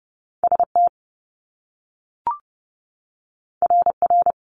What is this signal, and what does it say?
Morse code, heard as a keyed tone, about 30 wpm, 700 Hz. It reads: ST RR
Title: Top Two Letter Combinations in Words 30wpm